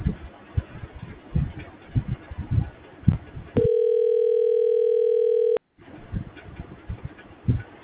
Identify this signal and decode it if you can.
Ringback tone (North America), 440 + 480 Hz.